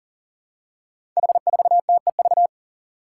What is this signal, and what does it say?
Morse code, heard as a keyed tone, about 38 wpm, 700 Hz. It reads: H4TEV
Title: Five Challenging Letter-Number Combinations 40wpm